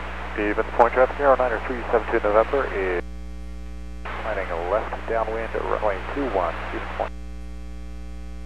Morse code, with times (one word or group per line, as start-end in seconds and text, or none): none